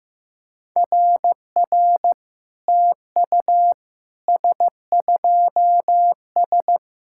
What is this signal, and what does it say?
Morse code, heard as a keyed tone, about 15 wpm, 700 Hz. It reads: RR TU S2S
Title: SOTA QSO Practice 15wpm